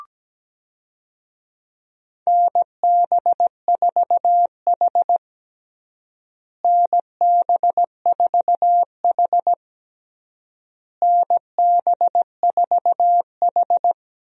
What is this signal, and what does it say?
Morse code, heard as a keyed tone, about 17 wpm, 700 Hz. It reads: NB4H NB4H NB4H
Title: USA Call Signs - 4 Characters - Repeated 3x - 17wpm